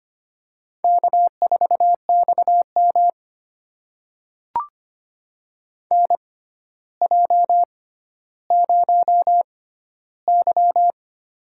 Morse code, heard as a keyed tone, about 25 wpm, spaced as for 13 wpm, 700 Hz. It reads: K4XM NJ0Y
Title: USA Call Signs - 4 Characters - 6x Character Spacing - 25wpm